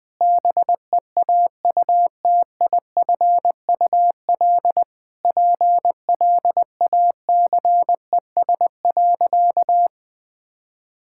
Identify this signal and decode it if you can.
Morse code, heard as a keyed tone, about 20 wpm, 700 Hz. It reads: BEAUTIFUL PLACES.
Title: Sentences with 3-Word Phrases - 20wpm